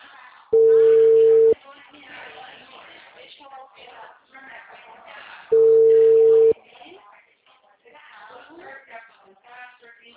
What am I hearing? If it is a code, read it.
Ringback tone (China, 450 Hz), with speech.